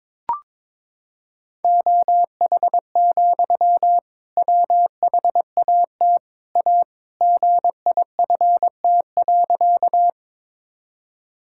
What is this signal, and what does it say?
Morse code, heard as a keyed tone, about 22 wpm, 700 Hz. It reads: OH, WHAT A GIFT.
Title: Sentences from Top 1000 Words - Review Part 6 - 22wpm